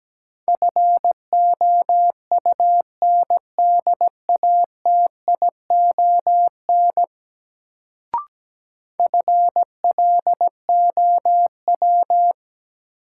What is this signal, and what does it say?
Morse code, heard as a keyed tone, about 17 wpm, 700 Hz. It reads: FOUNDATION FLOW
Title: Sentences from Top 1200 Words - New Word Drill Alt - 17wpm